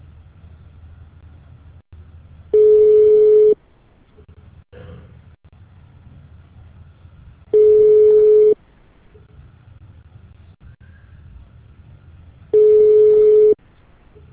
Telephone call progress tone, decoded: ringback tone

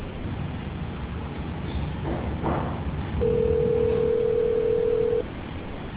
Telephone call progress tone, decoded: ringback tone

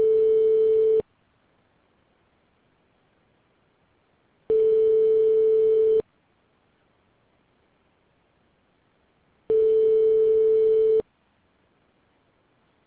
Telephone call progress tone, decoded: ringback tone

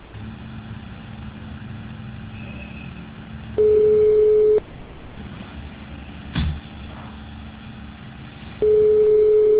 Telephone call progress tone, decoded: ringback tone

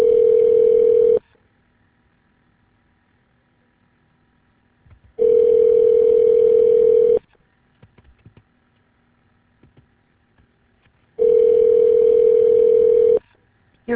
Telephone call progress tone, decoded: ringback tone

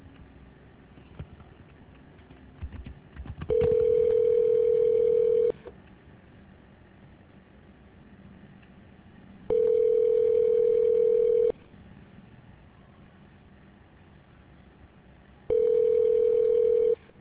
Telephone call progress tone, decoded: ringback tone

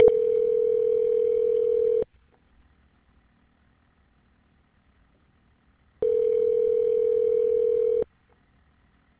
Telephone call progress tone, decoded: ringback tone

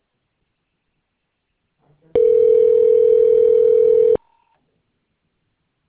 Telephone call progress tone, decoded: ringback tone

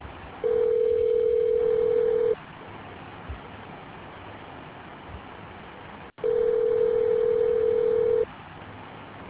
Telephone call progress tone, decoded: ringback tone

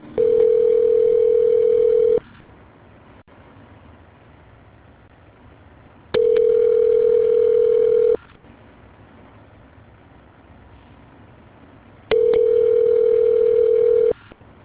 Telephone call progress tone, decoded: ringback tone

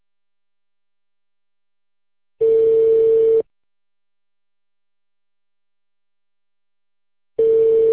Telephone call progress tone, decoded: ringback tone

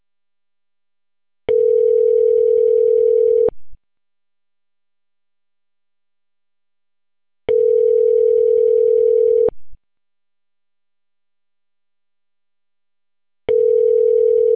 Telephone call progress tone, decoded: ringback tone

